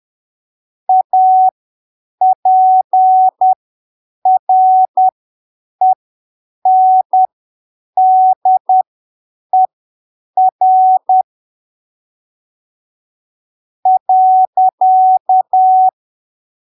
Morse code, heard as a keyed tone, about 10 wpm, spaced as for 7 wpm, 750 Hz. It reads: APRENDER .
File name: arquivos / texto / texto 21 10ppm